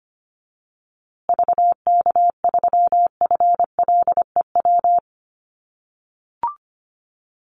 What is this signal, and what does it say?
Morse code, heard as a keyed tone, about 25 wpm, 700 Hz. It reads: VK3FLEW
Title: USA Call Signs Repeated 3x 25wpm